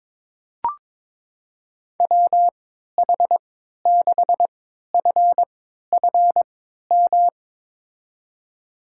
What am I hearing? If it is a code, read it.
Morse code, heard as a keyed tone, about 22 wpm, spaced as for 13 wpm, 700 Hz. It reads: WH6FFM
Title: USA Call Signs - 3x Character Spacing - 22wpm